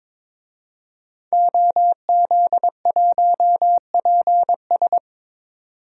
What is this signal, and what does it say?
Morse code, heard as a keyed tone, about 22 wpm, 700 Hz. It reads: OZ1PS
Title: DXCC-100 Call Signs - 5 Characters - 22wpm